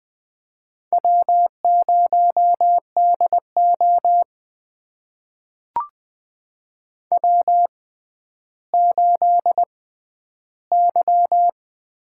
Morse code, heard as a keyed tone, about 20 wpm, spaced as for 11 wpm, 700 Hz. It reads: W0DO W8Y
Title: USA Call Signs - 4 Characters - 6x Character Spacing - 20wpm